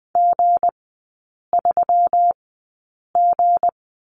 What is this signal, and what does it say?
Morse code, heard as a keyed tone, about 20 wpm, spaced as for 8 wpm, 690 Hz. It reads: G3G
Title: 159 Lesson 26e - Letter g - Three Characters 20wpm